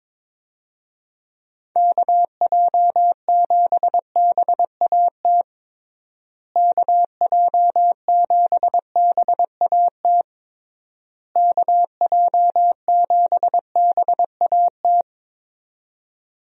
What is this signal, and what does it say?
Morse code, heard as a keyed tone, about 22 wpm, 700 Hz. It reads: KJ7BAT KJ7BAT KJ7BAT
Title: USA Call Signs - 6 Characters - Repeated 3x - 22wpm